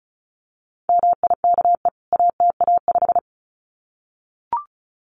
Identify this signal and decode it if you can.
Morse code, heard as a keyed tone, about 34 wpm, 700 Hz. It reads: MIKE ATA5
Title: CWT Exchange 35wpm